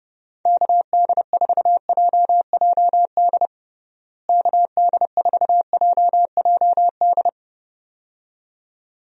Morse code, heard as a keyed tone, about 32 wpm, 700 Hz. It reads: KD4JJD KD4JJD
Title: USA Call Signs - 6 Characters - Repeated 2x - 30wpm